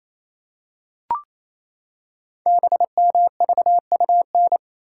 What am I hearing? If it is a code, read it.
Morse code, heard as a keyed tone, about 28 wpm, 700 Hz. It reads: BMVUN
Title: Code Groups - 5 Letters - 28wpm